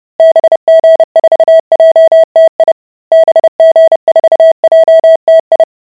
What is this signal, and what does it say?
Morse code, heard as a keyed tone, about 30 wpm, 650 Hz. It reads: BG4JTI BG4JTI